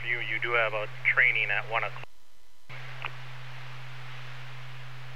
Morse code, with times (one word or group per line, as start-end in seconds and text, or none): none